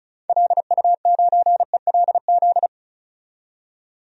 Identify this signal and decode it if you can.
Morse code, heard as a keyed tone, about 36 wpm, 700 Hz. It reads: LU9ELZ
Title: DXCC-100 Call Signs - 6 Characters - 35wpm